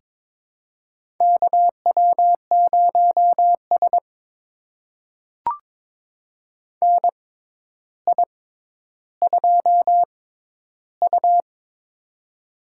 Morse code, heard as a keyed tone, about 22 wpm, spaced as for 11 wpm, 700 Hz. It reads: KW0S NI2U
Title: USA Call Signs - 4 Characters - 6x Character Spacing - 22wpm